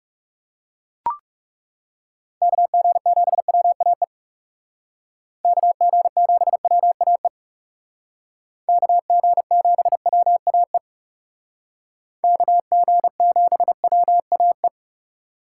Morse code, heard as a keyed tone, about 36 wpm, 700 Hz. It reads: KG7WAE KG7WAE KG7WAE KG7WAE
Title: Speed Racing - USA 2x3 Call Signs 30wpm